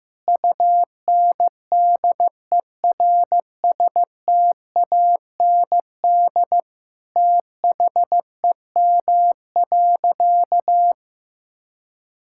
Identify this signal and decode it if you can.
Morse code, heard as a keyed tone, about 15 wpm, 700 Hz. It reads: UNDERSTAND THEM.
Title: Sentences from Top 1000 Words - Review Part 5 - 15wpm